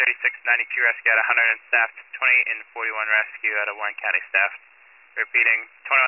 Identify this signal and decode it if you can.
Emergency Alert System (SAME) tones (header bursts), heard as harsh buzzing bursts, with speech. It is heard from 1.1 to 5.7 s.